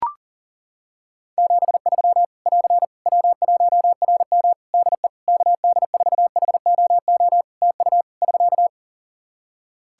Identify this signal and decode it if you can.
Morse code, heard as a keyed tone, about 40 wpm, 700 Hz. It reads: 73 <AR> W1RM DE KD4HOO TU <SK>
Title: Practice QSO - Exchange 4 - Ending - 40wpm